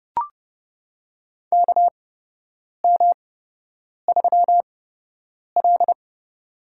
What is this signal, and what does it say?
Morse code, heard as a keyed tone, about 28 wpm, spaced as for 8 wpm, 700 Hz. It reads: KM3L
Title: USA Call Signs - 4 Characters - 8x Character Spacing - 30wpm